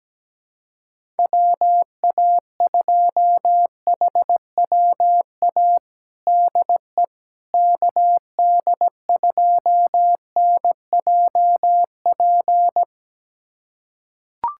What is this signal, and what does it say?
Morse code, heard as a keyed tone, about 17 wpm, 700 Hz. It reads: WA2HWA DE KD2NJP
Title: USA Callsign DE Callsign - 6 Characters - 17wpm